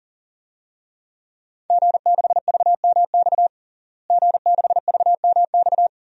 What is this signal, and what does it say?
Morse code, heard as a keyed tone, about 40 wpm, 700 Hz. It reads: G6VMX G6VMX
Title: DXCC-100 Call Signs - 5 Characters - Repeated 2x - 40wpm